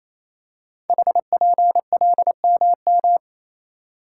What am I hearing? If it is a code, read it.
Morse code, heard as a keyed tone, about 27 wpm, 700 Hz. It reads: HPLMM
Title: Code Groups - 5 Letters - 28wpm